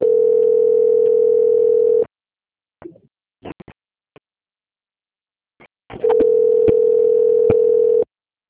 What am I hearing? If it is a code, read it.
Ringback tone (North America), 440 + 480 Hz.
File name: out- - - - - .10830